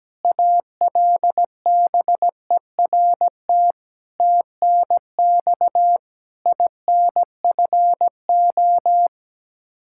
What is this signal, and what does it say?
Morse code, heard as a keyed tone, about 17 wpm, 700 Hz. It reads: ALBERT TNX INFO
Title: Practice QSO - Exchange 3 - More Info - 17wpm